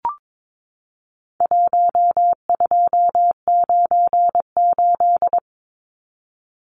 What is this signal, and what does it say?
Morse code, heard as a keyed tone, about 22 wpm, 700 Hz. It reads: 1298
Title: Numbers - 4-Digit 22wpm